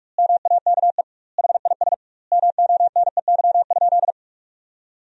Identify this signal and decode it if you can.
Morse code, heard as a keyed tone, about 42 wpm, 700 Hz. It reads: MAKE HIS MONEY?
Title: 2x Word-Spacing - Sentences from Top 400 Words - 45wpm